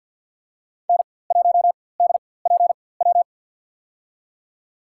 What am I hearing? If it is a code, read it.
Morse code, heard as a keyed tone, about 53 wpm, spaced as for 24 wpm, 700 Hz. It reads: N1DPW
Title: USA Call Signs - 4x Character Spacing - 50wpm